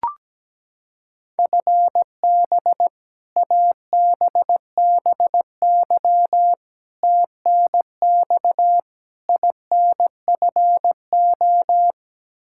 Morse code, heard as a keyed tone, about 17 wpm, 700 Hz. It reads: FB ABBY TNX INFO
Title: Practice QSO - Exchange 3 - More Info - 17wpm